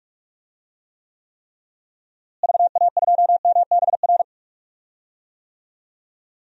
Morse code, heard as a keyed tone, about 44 wpm, 700 Hz. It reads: VA2MBR
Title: DXCC-100 Call Signs - 6 Characters - Repeated 3x - 45wpm